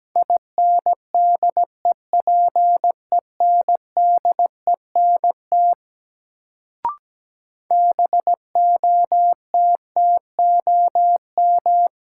Morse code, heard as a keyed tone, about 17 wpm, 700 Hz. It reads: INDEPENDENT BOTTOM
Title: Sentences from Top 900 Words - New Word Drill - 17wpm